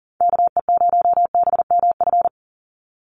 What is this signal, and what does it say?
Morse code, heard as a keyed tone, about 40 wpm, 700 Hz. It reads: KE0BMF
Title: USA Call Signs 40wpm